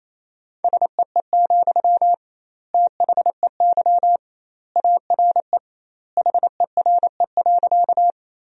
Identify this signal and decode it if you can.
Morse code, heard as a keyed tone, about 28 wpm, 700 Hz. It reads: SEE, THEY ARE HERE.
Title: 2x Word-Spacing - Sentences from Top 200 Words - 28wpm